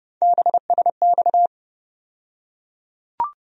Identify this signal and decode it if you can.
Morse code, heard as a keyed tone, about 30 wpm, 700 Hz. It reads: BSX
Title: USA Call Sign Postfixes - 3 Letters 30wpm